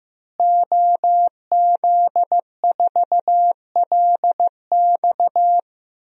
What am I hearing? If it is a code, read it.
Morse code, heard as a keyed tone, about 16 wpm, 700 Hz. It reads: OZ4LX
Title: DXCC-100 Call Signs - 5 Characters - Repeated 2x - 15wpm